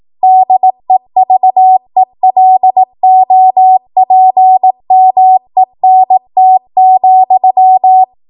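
Morse code, heard as a keyed tone, about 18 wpm, 750 Hz. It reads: DEVELOPMENT,
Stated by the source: ARRL 18 wpm Morse code practice archive transmission.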